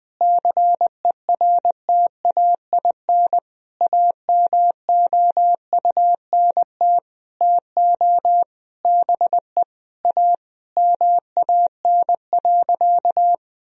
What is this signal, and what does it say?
Morse code, heard as a keyed tone, about 20 wpm, 700 Hz. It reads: CERTAIN AMOUNT TO BE A MAN.
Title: Sentences from Top 1000 Words - Review Part 4 - 20wpm